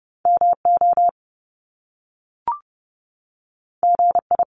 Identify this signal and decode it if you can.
Morse code, heard as a keyed tone, about 30 wpm, 700 Hz. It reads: MO GI
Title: Top Two Letter Combinations in Words 30wpm